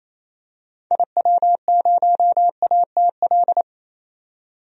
Morse code, heard as a keyed tone, about 28 wpm, 700 Hz. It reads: IW0ATL
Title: DXCC-100 Call Signs - 6 Characters - 28wpm